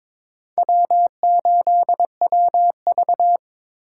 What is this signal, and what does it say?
Morse code, heard as a keyed tone, about 23 wpm, 700 Hz. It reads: W8WV
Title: USA Call Signs - 4 Characters - 6x Character Spacing - 22wpm